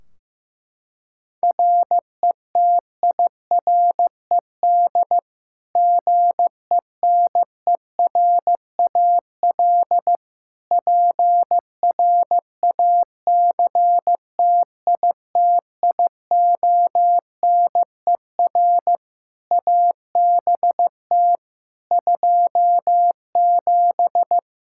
Morse code, heard as a keyed tone, about 15 wpm, 700 Hz. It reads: RETIRED GENERAL PRACTITIONER ABT 27
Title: Practice QSO - Exchange 3 - More Info - 15wpm